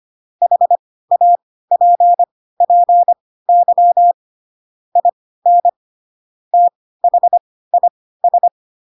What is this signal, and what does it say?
Morse code, heard as a keyed tone, about 25 wpm, spaced as for 16 wpm, 700 Hz. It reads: HAPPY IN THIS
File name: Pinocchio / Pinocchio ch4 10 25 ik1hge